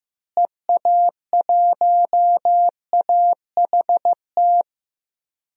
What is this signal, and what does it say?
Morse code, heard as a keyed tone, about 15 wpm, 700 Hz. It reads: EA1AHT
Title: DXCC-100 Call Signs - 6 Characters - Repeated 2x - 15wpm